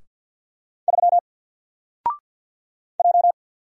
Morse code, heard as a keyed tone, about 48 wpm, 700 Hz. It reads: 3 J
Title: Single Letter-Number Alt 50wpm